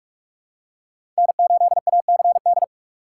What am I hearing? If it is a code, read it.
Morse code, heard as a keyed tone, about 41 wpm, 700 Hz. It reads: N8ACD
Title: USA Call Signs - 4x Character Spacing - 45wpm